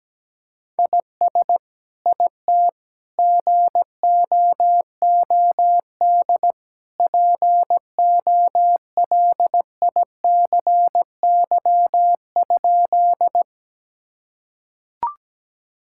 Morse code, heard as a keyed tone, about 17 wpm, 700 Hz. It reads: IS IT GOOD POLICY?
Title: Sentences from Top 1000 Words - Review Part 5 - 17wpm